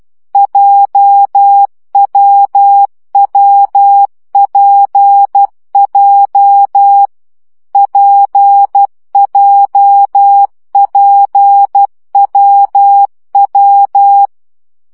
Morse code, heard as a keyed tone, about 12 wpm, 800 Hz. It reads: JWWPJ PJPWW
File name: CW gyakorlas w p j 10 baud